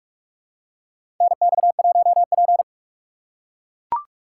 Morse code, heard as a keyed tone, about 41 wpm, 700 Hz. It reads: NX1P
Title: USA Call Signs - 4 Characters - 3x Character Spacing - 45wpm